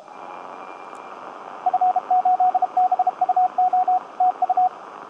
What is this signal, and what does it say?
Morse code, heard as a keyed tone, about 32 wpm, 750 Hz, with radio noise. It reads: F8BUO TU
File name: tt8tt-10m-cw-here-in-jn90ab